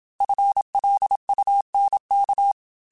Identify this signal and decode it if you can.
Morse code, heard as a keyed tone, about 27 wpm, 800 Hz. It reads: FLUNK